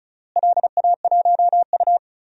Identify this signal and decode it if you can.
Morse code, heard as a keyed tone, about 34 wpm, 700 Hz. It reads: LA1U
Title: DXCC-100 Call Signs - 4 Characters - Repeated 3x - 35wpm